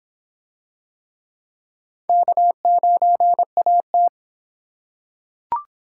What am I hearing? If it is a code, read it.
Morse code, heard as a keyed tone, about 26 wpm, 700 Hz. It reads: K9AT